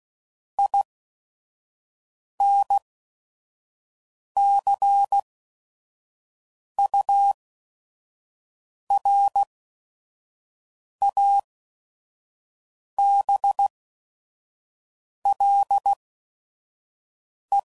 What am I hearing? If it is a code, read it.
Morse code, heard as a keyed tone, about 16 wpm, spaced as for 5 wpm, 800 Hz. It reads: INCURABLE
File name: Sayings / Quotes / Optimist